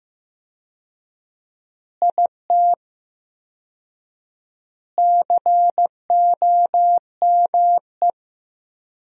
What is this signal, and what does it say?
Morse code, heard as a keyed tone, about 15 wpm, 700 Hz. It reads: IT COME
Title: 4x Word-Spacing - Sentences from Top 100 Words - 15wpm